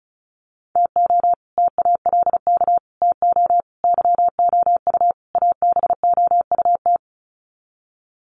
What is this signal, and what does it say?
Morse code, heard as a keyed tone, about 34 wpm, 700 Hz. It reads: TO TALK TO YOU ABOUT
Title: Sets of 5 Words Encore 35wpm